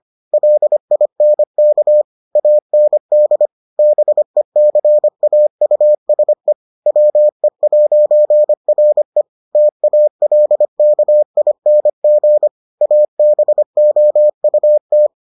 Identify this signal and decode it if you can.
Morse code, heard as a keyed tone, about 25 wpm, 600 Hz. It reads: LINK AND BECAUSE WE'RE TALKING ABOUT